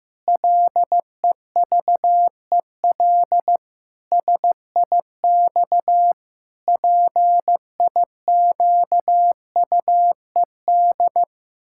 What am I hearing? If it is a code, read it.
Morse code, heard as a keyed tone, about 15 wpm, 700 Hz. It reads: LEVEL SIX PIQUED
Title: Pangram Challenge v2 - 15wpm